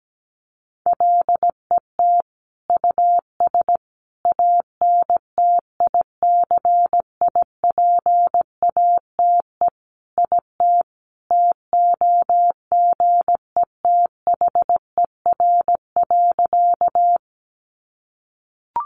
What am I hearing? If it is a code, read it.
Morse code, heard as a keyed tone, about 17 wpm, 700 Hz. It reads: LET US ANTICIPATE IT TOGETHER.
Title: Sentences from Top 1500 Words 17wpm